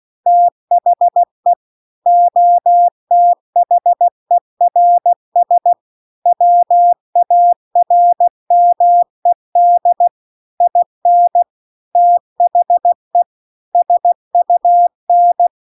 Morse code, heard as a keyed tone, about 16 wpm, 700 Hz. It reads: THE OTHERS WARMED IN THE SUN
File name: CW-The-Big-Twoheaded-River-2-16WPM